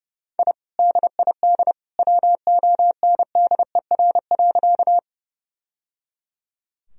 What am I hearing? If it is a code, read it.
Morse code, heard as a keyed tone, about 30 wpm, 700 Hz. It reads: I DID WONDER.
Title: Sentences from Top 1000 Words - Review Part 3 - 30wpm